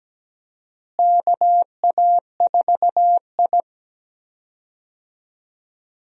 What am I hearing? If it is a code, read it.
Morse code, heard as a keyed tone, about 17 wpm, 700 Hz. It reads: KA4I
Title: USA Call Signs - 4 Characters - Repeated 3x - 17wpm